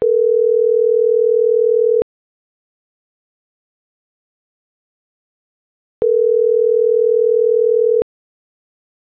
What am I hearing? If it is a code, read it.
Ringback tone (North America), 440 + 480 Hz.